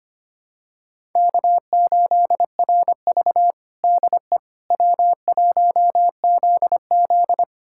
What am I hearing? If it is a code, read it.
Morse code, heard as a keyed tone, about 25 wpm, 700 Hz. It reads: K8RV DE W1ZZ
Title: USA Callsign DE Callsign - 4 Characters - 25wpm